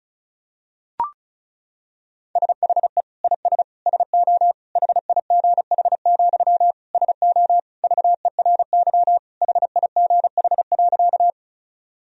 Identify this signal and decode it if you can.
Morse code, heard as a keyed tone, about 36 wpm, 690 Hz. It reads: SHE IS SO HIGH, SO VERY HIGH.
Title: Sentences from Top 1000 Words - Review Part 1 - 35wpm